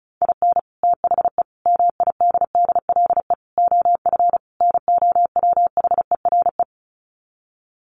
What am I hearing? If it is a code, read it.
Morse code, heard as a keyed tone, about 34 wpm, 700 Hz. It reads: IN THE MIDDLE OF NOWHERE
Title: Sets of 5 Words Encore 35wpm